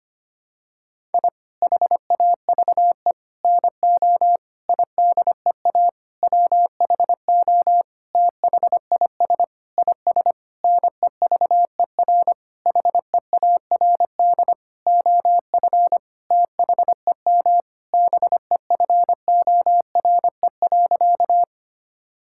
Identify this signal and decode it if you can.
Morse code, heard as a keyed tone, about 25 wpm, 700 Hz. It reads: I HAVE NO IDEA WHO THIS IS NEVER HEARD OF THEM BEFORE.
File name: practice / 25wpm / Sentences with 9-Word Phrases - 25wpm